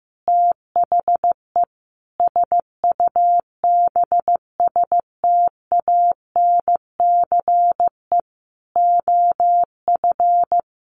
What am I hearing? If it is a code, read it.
Morse code, heard as a keyed tone, about 15 wpm, 700 Hz. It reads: THE SUBSTANCE OF
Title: Sentences from Top 1500 Words 15wpm